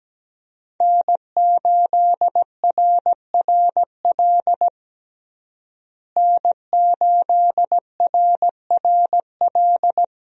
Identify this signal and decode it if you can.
Morse code, heard as a keyed tone, about 17 wpm, 700 Hz. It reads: N8RRL N8RRL
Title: USA Call Signs - 5 Characters - Repeated 2x - 17wpm